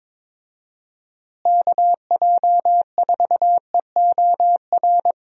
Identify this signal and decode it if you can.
Morse code, heard as a keyed tone, about 21 wpm, 700 Hz. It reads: KJ4EOR
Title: USA Call Signs - 6 Characters - Repeated 2x - 22wpm